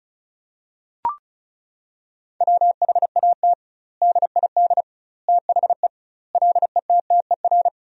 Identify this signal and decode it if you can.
Morse code, heard as a keyed tone, about 34 wpm, 700 Hz. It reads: WHAT DID THE LETTER
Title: 2x Word-Spacing - Sentences from Top 500 Words - 35wpm